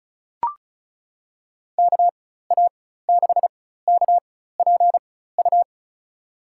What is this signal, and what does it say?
Morse code, heard as a keyed tone, about 36 wpm, spaced as for 16 wpm, 700 Hz. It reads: KA6KPU
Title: USA Call Signs - 6 Characters - 4x Character Spacing - 35wpm